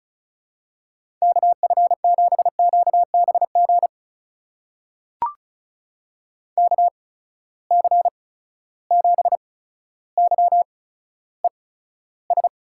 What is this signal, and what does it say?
Morse code, heard as a keyed tone, about 36 wpm, spaced as for 15 wpm, 700 Hz. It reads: KF7QBG KC7YES
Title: USA Call Signs - 6 Characters - 8x Character Spacing - 35wpm